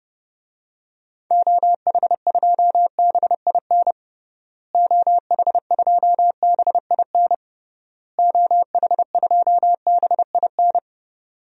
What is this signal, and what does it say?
Morse code, heard as a keyed tone, about 32 wpm, 700 Hz. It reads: OH2BIN OH2BIN OH2BIN
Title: DXCC-100 Call Signs - 6 Characters - Repeated 3x - 30wpm